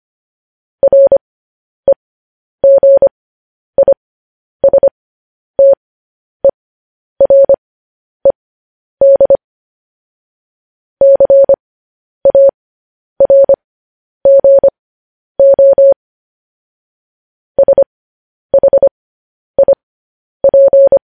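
Morse code, heard as a keyed tone, about 25 wpm, spaced as for 10 wpm, 550 Hz. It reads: REGISTERED CARGO SHIP
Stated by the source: Morse code news headlines practice transmission.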